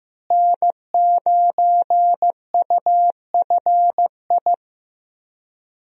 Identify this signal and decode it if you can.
Morse code, heard as a keyed tone, about 15 wpm, 700 Hz. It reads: N9UFI
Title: USA Call Signs - 5 Characters - Repeated 3x - 15wpm